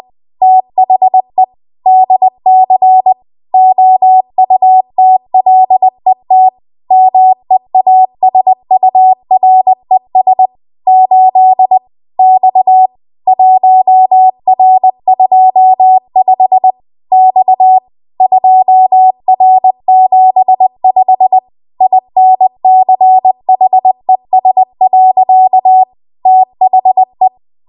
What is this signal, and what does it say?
Morse code, heard as a keyed tone, about 20 wpm, 750 Hz. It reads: THE DC OUTLET MEASURES 8 X 1R25 X 2R75 INCHES. THE
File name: Morse / 20WPM